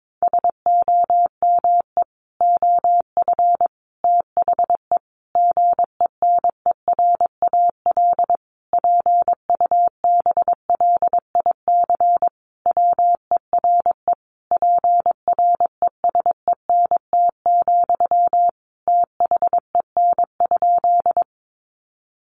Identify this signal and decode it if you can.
Morse code, heard as a keyed tone, about 22 wpm, 700 Hz. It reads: SOME OF THE GENERAL PUBLIC WERE PRESENT, THEN?
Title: Sentences from Top 800 Words 22wpm